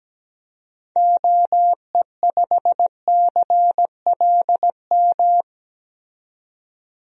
Morse code, heard as a keyed tone, about 17 wpm, 700 Hz. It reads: OE5CLM
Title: DXCC-100 Call Signs - 6 Characters - 17wpm